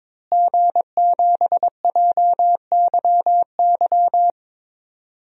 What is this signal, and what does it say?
Morse code, heard as a keyed tone, about 22 wpm, 700 Hz. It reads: G7JYY
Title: DXCC-100 Call Signs - 5 Characters - Repeated 2x - 22wpm